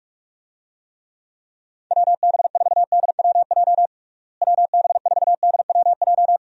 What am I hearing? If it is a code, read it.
Morse code, heard as a keyed tone, about 42 wpm, 700 Hz. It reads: WB4DWJ WB4DWJ
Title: USA Call Signs - 6 Characters - Repeated 2x - 45wpm